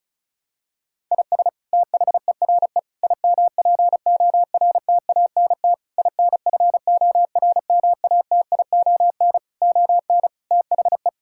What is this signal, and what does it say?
Morse code, heard as a keyed tone, about 36 wpm, 700 Hz. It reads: IS THERE IMPORTANT INFORMATION ON THE
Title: Sentences from Top 1000 Words - Review Part 6 - 35wpm